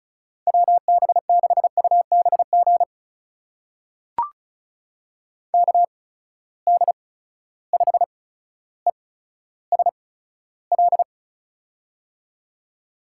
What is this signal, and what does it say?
Morse code, heard as a keyed tone, about 36 wpm, spaced as for 15 wpm, 700 Hz. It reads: WB6UBG KD5ESL
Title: USA Call Signs - 6 Characters - 8x Character Spacing - 35wpm